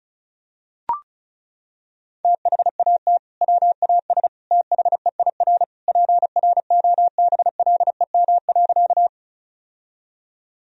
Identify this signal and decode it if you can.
Morse code, heard as a keyed tone, about 34 wpm, 700 Hz. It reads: THAT WAS THEIR PROBLEM.
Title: Sentences from Top 1000 Words - Review Part 2 - 35wpm